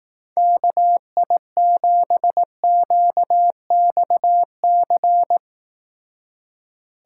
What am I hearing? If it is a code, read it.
Morse code, heard as a keyed tone, about 18 wpm, 700 Hz. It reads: KI7QXC
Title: Speed Racing - USA 2x3 Call Signs 15wpm